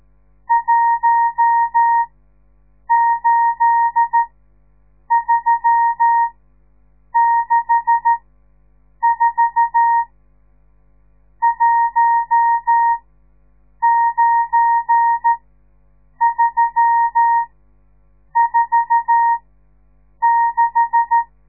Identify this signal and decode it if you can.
Morse code, heard as a keyed tone, about 13 wpm, spaced as for 7 wpm, 930 Hz. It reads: 18364 19346
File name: radiomorningstar / morse / les 10